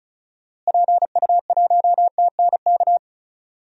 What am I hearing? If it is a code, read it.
Morse code, heard as a keyed tone, about 36 wpm, 700 Hz. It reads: PU1TNK
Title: DXCC-100 Call Signs - 6 Characters - Repeated 3x - 35wpm